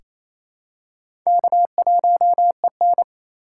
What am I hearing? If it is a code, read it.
Morse code, heard as a keyed tone, about 28 wpm, 700 Hz. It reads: K1EN
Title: USA Call Signs - 6x Character Spacing - 28wpm